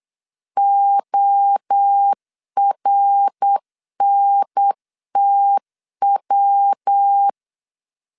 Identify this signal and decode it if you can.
Morse code, heard as a keyed tone, about 8 wpm, 800 Hz. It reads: ORNTW
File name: CW / DET / 1 3 1500C S B 8wpm 4141